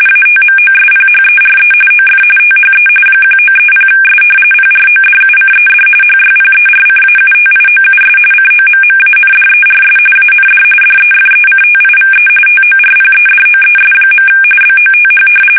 RTTY, about 45.45 baud. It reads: ATMOSPHERE OF THE 1930S PRODUCED HYPERCAUTIOUS COMMANDERS AND DEFECTIVE TORPEDOES. BOTH
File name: fsk850